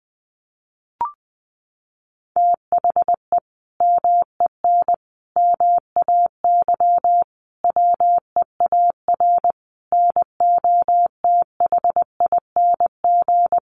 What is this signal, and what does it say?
Morse code, heard as a keyed tone, about 20 wpm, 700 Hz. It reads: THE MEN MAY WEAR NOTHING
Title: Sentences from Top 1500 Words 20wpm